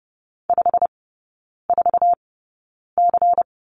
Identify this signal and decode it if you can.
Morse code, heard as a keyed tone, about 30 wpm, spaced as for 9 wpm, 700 Hz. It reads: 54C
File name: Lesson 16e - Letter c - Three Characters 30wpm